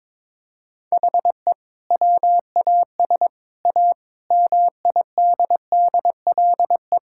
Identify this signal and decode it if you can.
Morse code, heard as a keyed tone, about 22 wpm, 700 Hz. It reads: HE WAS A MIDDLE
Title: Sentences from Top 1000 Words - Review Part 6 - 22wpm